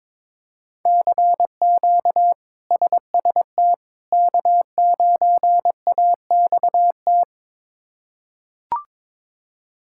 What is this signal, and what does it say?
Morse code, heard as a keyed tone, about 22 wpm, 700 Hz. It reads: CQ SST K9AXT
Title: K1USN SST QSO Practice - 22wpm